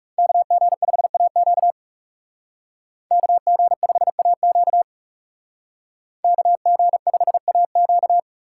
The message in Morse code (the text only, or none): KG5AQ KG5AQ KG5AQ